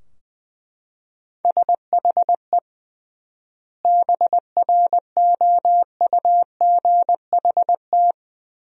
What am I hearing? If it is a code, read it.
Morse code, heard as a keyed tone, about 21 wpm, 700 Hz. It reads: SHE BROUGHT
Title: 3x Word-Spacing - Sentences from Top 500 Words - 20wpm